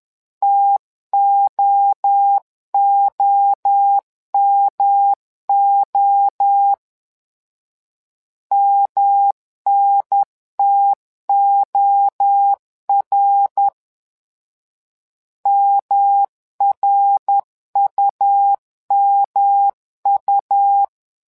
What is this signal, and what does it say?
Morse code, heard as a keyed tone, about 11 wpm, 800 Hz. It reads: TOOMO MNTOR MRUMU
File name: LEZIONI / 3 MWRO / 3 MWRO 10wpm 2148